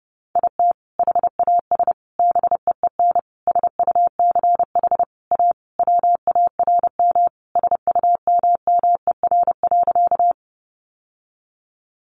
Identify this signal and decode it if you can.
Morse code, heard as a keyed tone, about 30 wpm, 700 Hz. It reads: IT HAS BEEN SUCH A WARM SUMMER.
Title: Sentences from Top 1100 Words 30wpm